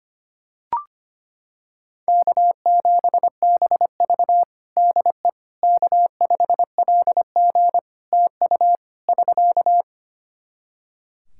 Morse code, heard as a keyed tone, about 25 wpm, 700 Hz. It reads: K7BV DE K5LG TU <SK>
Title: K1USN SST QSO Practice 25wpm